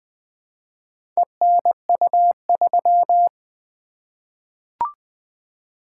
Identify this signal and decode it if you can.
Morse code, heard as a keyed tone, about 20 wpm, 700 Hz. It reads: ENU3
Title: Four Letter-Number Combinations - 20wpm